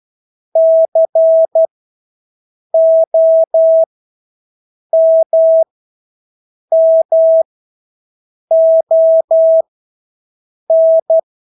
Morse code, heard as a keyed tone, about 12 wpm, spaced as for 6 wpm, 650 Hz. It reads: COMMON
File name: Engelske ord 12-6 wpm